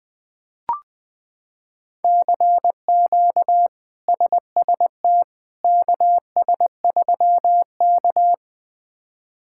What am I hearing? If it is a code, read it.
Morse code, heard as a keyed tone, about 21 wpm, 700 Hz. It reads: CQ SST KS3K
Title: K1USN SST QSO Practice - 20wpm